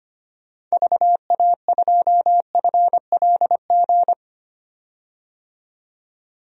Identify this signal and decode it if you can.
Morse code, heard as a keyed tone, about 25 wpm, 700 Hz. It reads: VA2FLG
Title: DXCC-100 Call Signs - 6 Characters - 25wpm